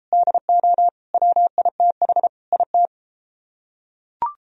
Morse code, heard as a keyed tone, about 30 wpm, 700 Hz. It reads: DO WITH IT